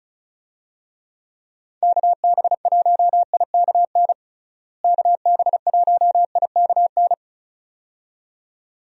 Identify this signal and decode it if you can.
Morse code, heard as a keyed tone, about 36 wpm, 700 Hz. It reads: KB1IKN KB1IKN
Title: USA Call Signs - 6 Characters - Repeated 2x - 35wpm